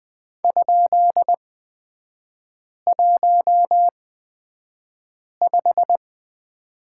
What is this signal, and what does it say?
Morse code, heard as a keyed tone, about 20 wpm, spaced as for 5 wpm, 700 Hz. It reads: ?15
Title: 236 Lesson 38e - Letter q - Three Characters 20wpm